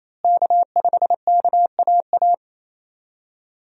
Morse code, heard as a keyed tone, about 28 wpm, 700 Hz. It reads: K5KAA
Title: USA Call Signs - 5 Characters - Repeated 2x - 28wpm